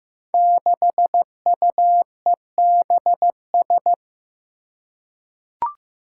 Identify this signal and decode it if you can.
Morse code, heard as a keyed tone, about 15 wpm, 700 Hz. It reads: 6UEBS
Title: Five Challenging Letter-Number Combinations 15wpm